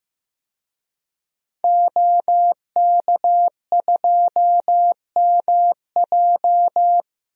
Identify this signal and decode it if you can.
Morse code, heard as a keyed tone, about 15 wpm, 700 Hz. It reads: OK2MJ
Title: DXCC-100 Call Signs - 5 Characters - 15wpm